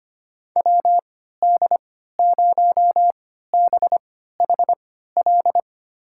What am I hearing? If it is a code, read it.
Morse code, heard as a keyed tone, about 25 wpm, spaced as for 14 wpm, 700 Hz. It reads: WD0BHL
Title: USA Call Signs - 6 Characters - 3x Character Spacing - 25wpm